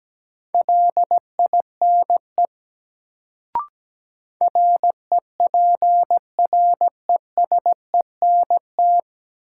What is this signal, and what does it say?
Morse code, heard as a keyed tone, about 17 wpm, 700 Hz. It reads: LINE REPRESENT
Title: Sentences from Top 1200 Words - New Word Drill Alt - 17wpm